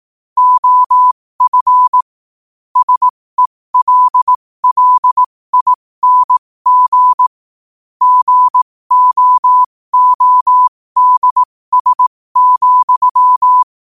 Morse code, heard as a keyed tone, about 18 wpm, spaced as for 15 wpm, 1000 Hz. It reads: OF SELLING GOODS,
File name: qotdfiles / 15wpm